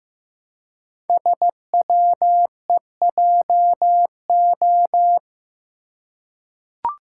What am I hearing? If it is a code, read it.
Morse code, heard as a keyed tone, about 15 wpm, 700 Hz. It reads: SWEJO
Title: Code Groups - 5 Letters - 15wpm